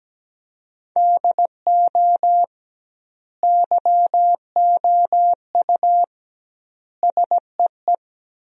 Morse code, heard as a keyed tone, about 17 wpm, 700 Hz. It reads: DO YOU SEE
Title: 2x Word-Spacing - Sentences from Top 500 Words - 17wpm